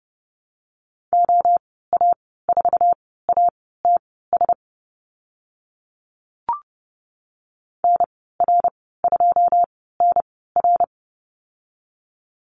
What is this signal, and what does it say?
Morse code, heard as a keyed tone, about 30 wpm, spaced as for 17 wpm, 700 Hz. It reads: OA4ATS NR2NR
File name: Lesson 14g - Number 2 - Call Signs 30wpm